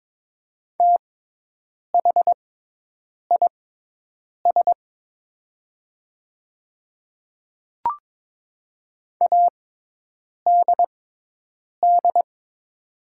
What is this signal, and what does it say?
Morse code, heard as a keyed tone, about 22 wpm, spaced as for 8 wpm, 700 Hz. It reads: THIS ADD
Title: 6x Character Spacing - Top 300 Words - 22wpm